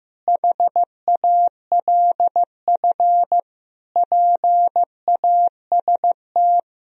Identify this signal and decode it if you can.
Morse code, heard as a keyed tone, about 15 wpm, 700 Hz. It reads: HALF PAST
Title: Sentences from Top 1000 Words - Review Part 3 - 15wpm